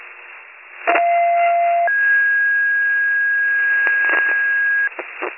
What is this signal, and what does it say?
Two-tone page, heard as a one-second tone, then a three-second tone, about 690 then 1650 Hz, with noise. It is heard from 0.9 to 4.9 s.